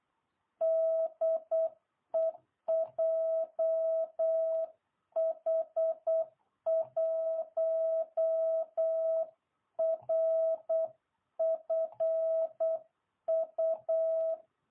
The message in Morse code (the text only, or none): DE JH1RFU